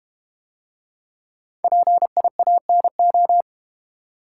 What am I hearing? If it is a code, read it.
Morse code, heard as a keyed tone, about 32 wpm, 700 Hz. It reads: PIANO